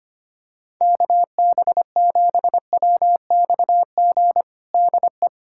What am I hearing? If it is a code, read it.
Morse code, heard as a keyed tone, about 25 wpm, 700 Hz. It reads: KB7WXG DE
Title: USA Callsign DE Callsign 25wpm